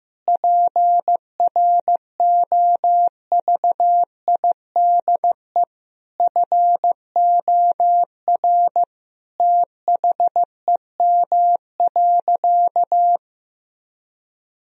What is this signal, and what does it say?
Morse code, heard as a keyed tone, about 16 wpm, 700 Hz. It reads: PROVIDE FOR THEM.
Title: Sentences from Top 1000 Words - Review Part 3 - 15wpm